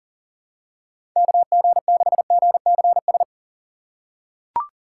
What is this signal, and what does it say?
Morse code, heard as a keyed tone, about 40 wpm, 700 Hz. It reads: KG6GCS
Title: USA Call Signs - 4x Character Spacing - 40wpm